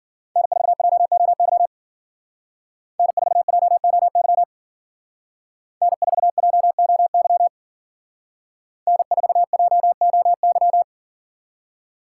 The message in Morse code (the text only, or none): N4JOY N4JOY N4JOY N4JOY